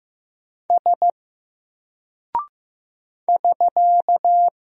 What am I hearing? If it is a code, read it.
Morse code, heard as a keyed tone, about 15 wpm, 700 Hz. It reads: S <SK>
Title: Single Character (Letter:Number:Punctuation:Prosign) Alt 15wpm